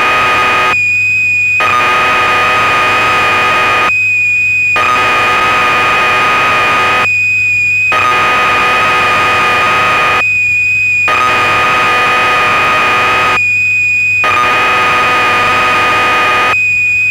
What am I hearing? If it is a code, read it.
Cassette data (Kansas City type).
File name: Acorn User/1984-12 Electron Programs